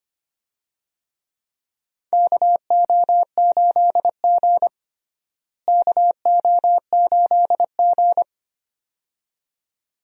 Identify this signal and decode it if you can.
Morse code, heard as a keyed tone, about 25 wpm, 700 Hz. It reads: KO8G KO8G
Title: USA Call Signs - 4 Characters - Repeated 2x - 25wpm